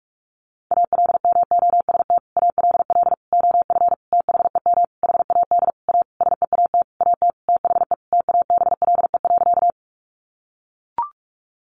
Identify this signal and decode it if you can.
Morse code, heard as a keyed tone, about 44 wpm, 700 Hz. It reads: ALMOST ALL OF THEM HAD A SEAT AT THE TABLE.
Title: Sentences from Top 1500 Words 45wpm